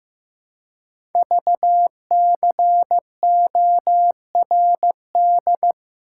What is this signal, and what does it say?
Morse code, heard as a keyed tone, about 15 wpm, 700 Hz. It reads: VCORD
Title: Code Groups - 5 Letters - 15wpm